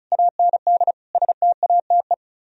Morse code, heard as a keyed tone, about 37 wpm, 700 Hz. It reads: AND STATE
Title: Speed-Racing - Sets of 5 Words 30wpm